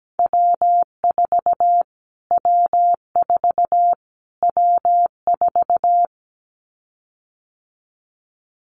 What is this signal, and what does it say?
Morse code, heard as a keyed tone, about 17 wpm, 700 Hz. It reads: W4 W4 W4
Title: Callsign Prefixes for Top 25 DXCC - Repeated 3x - 17wpm